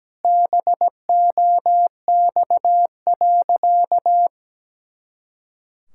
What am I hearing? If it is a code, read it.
Morse code, heard as a keyed tone, about 17 wpm, 700 Hz. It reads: BOX.
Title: Pangram Challenge v2 - 17wpm